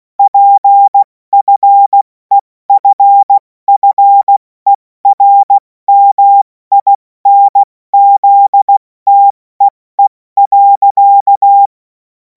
Morse code, heard as a keyed tone, about 16 wpm, spaced as for 14 wpm, 800 Hz. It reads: PFEFFERMINZTEE.